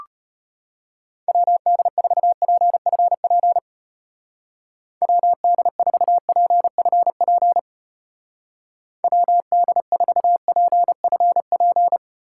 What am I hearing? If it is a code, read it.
Morse code, heard as a keyed tone, about 32 wpm, 700 Hz. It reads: WD4PFP WD4PFP WD4PFP